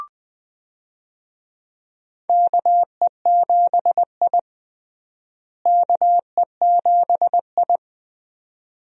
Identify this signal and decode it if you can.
Morse code, heard as a keyed tone, about 21 wpm, 700 Hz. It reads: KE7I KE7I
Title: USA Call Signs - 4 Characters - Repeated 2x - 20wpm